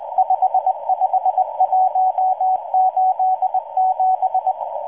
Morse code, heard as a keyed tone, about 19 wpm, 750 Hz, with radio noise. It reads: 55187